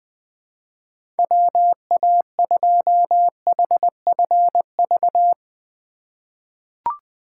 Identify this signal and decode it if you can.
Morse code, heard as a keyed tone, about 20 wpm, 700 Hz. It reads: WA2HFV
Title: USA Call Signs - 6 Characters - 4x Character Spacing - 20wpm